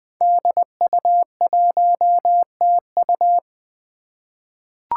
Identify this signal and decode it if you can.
Morse code, heard as a keyed tone, about 20 wpm, 700 Hz. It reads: DU1TU
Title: Five Challenging Letter-Number Combinations 20wpm